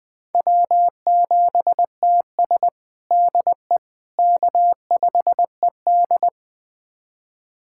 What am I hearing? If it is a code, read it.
Morse code, heard as a keyed tone, about 20 wpm, 700 Hz. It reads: W7TS DE K5ED
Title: USA Callsign DE Callsign - 4 Characters - 20wpm